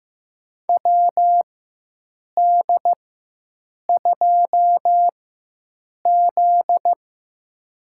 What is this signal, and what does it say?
Morse code, heard as a keyed tone, about 15 wpm, spaced as for 7 wpm, 700 Hz. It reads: WD2Z
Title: USA Call Signs - 4 Characters - 4x Character Spacing - 15wpm